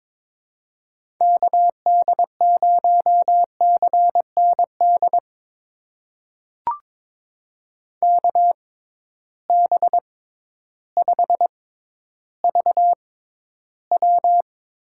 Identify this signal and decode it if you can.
Morse code, heard as a keyed tone, about 22 wpm, spaced as for 12 wpm, 700 Hz. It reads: KD0CND KB5VW
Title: USA Call Signs - 6 Characters - 6x Character Spacing - 22wpm